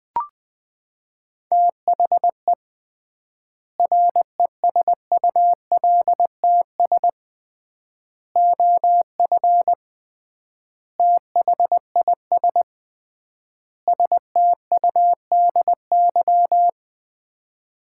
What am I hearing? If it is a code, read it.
Morse code, heard as a keyed tone, about 20 wpm, 700 Hz. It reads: THE RESULTS OF THIS STUDY
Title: 3x Word-Spacing - Sets of 5 Words - 20wpm